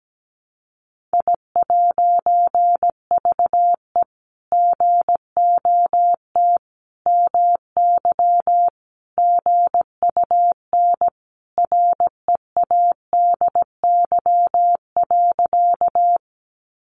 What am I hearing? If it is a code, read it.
Morse code, heard as a keyed tone, about 17 wpm, 700 Hz. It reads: I'VE GOT MY GUN READY.
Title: A Sound Of Thunder 17wpm